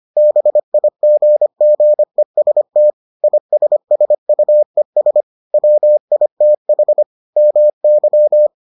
Morse code, heard as a keyed tone, about 25 wpm, 600 Hz. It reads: BIGGEST ISSUES WITH MY